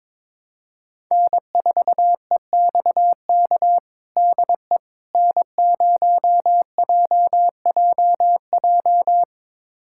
Morse code, heard as a keyed tone, about 22 wpm, 700 Hz. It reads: N4EXK DE N0JJJ
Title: USA Callsign DE Callsign - 5 Characters - 22wpm